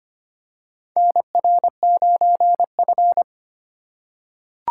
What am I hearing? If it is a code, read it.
Morse code, heard as a keyed tone, about 25 wpm, 700 Hz. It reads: NR9F
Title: USA Call Signs - 4 Characters - 2x Character Spacing - 25wpm